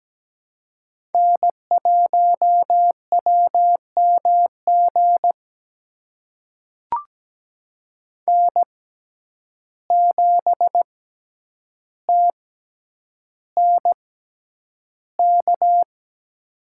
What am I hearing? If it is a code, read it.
Morse code, heard as a keyed tone, about 17 wpm, spaced as for 9 wpm, 700 Hz. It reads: N1WMG N7TNK
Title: USA Call Signs - 5 Characters - 6x Character Spacing - 17wpm